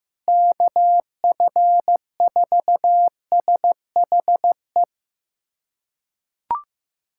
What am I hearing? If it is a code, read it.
Morse code, heard as a keyed tone, about 15 wpm, 700 Hz. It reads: KF4SHE
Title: USA Callsign DE Callsign 15wpm